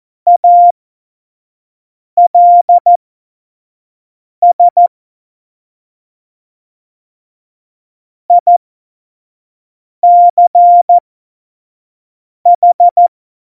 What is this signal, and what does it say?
Morse code, heard as a keyed tone, about 14 wpm, spaced as for 5 wpm, 700 Hz. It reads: ALS ICH
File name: 25 F70